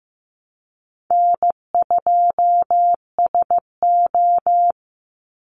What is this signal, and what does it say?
Morse code, heard as a keyed tone, about 15 wpm, 700 Hz. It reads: N2SO
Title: Easy Call Signs Repeated 2x - 15wpm